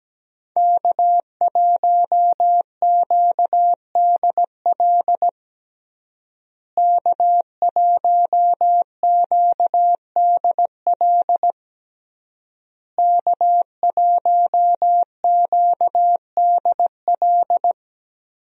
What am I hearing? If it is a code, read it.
Morse code, heard as a keyed tone, about 17 wpm, 700 Hz. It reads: K1QDL K1QDL K1QDL
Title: USA Call Signs - 5 Characters - Repeated 3x - 17wpm